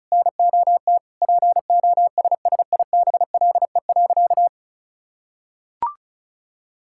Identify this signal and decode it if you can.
Morse code, heard as a keyed tone, about 36 wpm, 700 Hz. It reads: NOT POSSIBLE.
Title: Sentences from Top 1000 Words - Review Part 6 - 35wpm